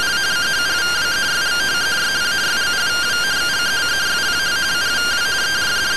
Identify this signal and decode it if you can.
Two-tone teleprinter signal, about 170 Hz shift.
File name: findings / CT AXIAL -256.gen-21 rtty rx